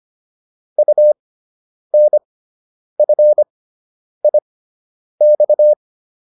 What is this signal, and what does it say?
Morse code, heard as a keyed tone, about 25 wpm, spaced as for 9 wpm, 600 Hz. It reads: UNFIX